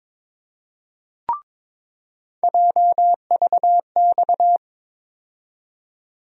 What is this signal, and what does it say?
Morse code, heard as a keyed tone, about 22 wpm, 700 Hz. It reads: JVX
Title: USA Call Sign Postfixes - 3 Letters 22wpm